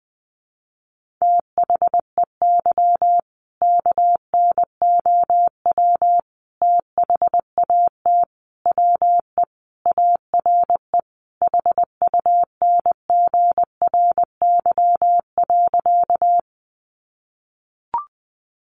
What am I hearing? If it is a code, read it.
Morse code, heard as a keyed tone, about 20 wpm, 700 Hz. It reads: THEY KNOW THAT WE ARE HUNGRY.
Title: Sentences from Top 1400 Words 20wpm